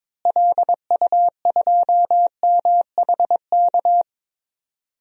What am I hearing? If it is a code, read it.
Morse code, heard as a keyed tone, about 22 wpm, 700 Hz. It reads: LU2MHK
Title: DXCC-100 Call Signs - 6 Characters - Repeated 3x - 22wpm